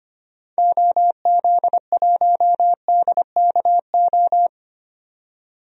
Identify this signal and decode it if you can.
Morse code, heard as a keyed tone, about 25 wpm, 700 Hz. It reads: OZ1DKO
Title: DXCC-100 Call Signs - 6 Characters - 25wpm